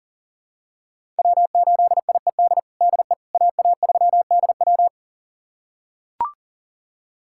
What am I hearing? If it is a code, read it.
Morse code, heard as a keyed tone, about 37 wpm, 700 Hz. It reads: W8IED DE AA3DW
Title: USA Callsign DE Callsign - 5 Characters - 40wpm